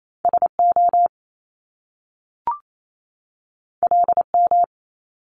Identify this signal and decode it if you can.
Morse code, heard as a keyed tone, about 28 wpm, 700 Hz. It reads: S O L M
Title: Top Two Letter Combinations in Words 28wpm